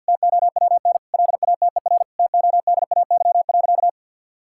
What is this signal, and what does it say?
Morse code, heard as a keyed tone, about 50 wpm, 700 Hz. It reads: TOWN LATER TODAY.
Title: Sentences from Top 1000 Words - Review Part 6 - 50wpm